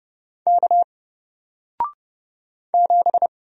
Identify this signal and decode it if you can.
Morse code, heard as a keyed tone, about 29 wpm, 700 Hz. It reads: K 7
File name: Single-Letter-Number-30wpm